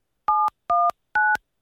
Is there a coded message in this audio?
Yes: DTMF from 0.3 to 1.4 s, heard as telephone touch tones, tones of about 230 ms. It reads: 719